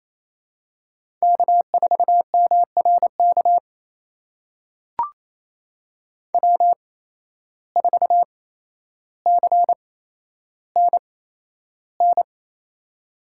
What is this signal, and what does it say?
Morse code, heard as a keyed tone, about 28 wpm, spaced as for 12 wpm, 700 Hz. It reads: K4MRK W4CNN
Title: USA Call Signs - 5 Characters - 8x Character Spacing - 28wpm